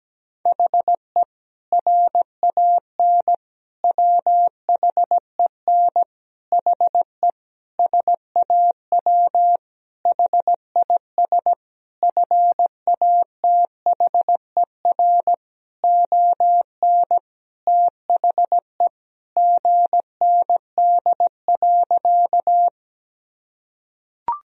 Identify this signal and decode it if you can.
Morse code, heard as a keyed tone, about 17 wpm, 700 Hz. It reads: HE RAN WHEN HE SAW HIS FATHER ON THE GND.
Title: Sentences from Top 500 Words and Abbreviations 17wpm